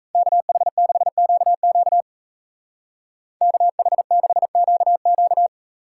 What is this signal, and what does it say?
Morse code, heard as a keyed tone, about 33 wpm, 700 Hz. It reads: KH6QQ KH6QQ
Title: Speed Racing - USA 2x2 Call Signs 28wpm